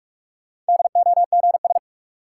Morse code, heard as a keyed tone, about 43 wpm, 700 Hz. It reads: DOGS
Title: Sentences from Top 1600 Words - New Word Drill - 45wpm